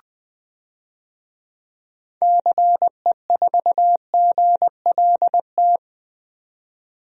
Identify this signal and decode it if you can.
Morse code, heard as a keyed tone, about 21 wpm, 700 Hz. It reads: CE4GLT
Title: DXCC-100 Call Signs - 6 Characters - 20wpm